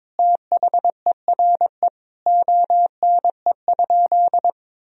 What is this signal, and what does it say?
Morse code, heard as a keyed tone, about 21 wpm, 700 Hz. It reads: THERE ONE?
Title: Sentences from Top 100 Words and Abbreviations 22wpm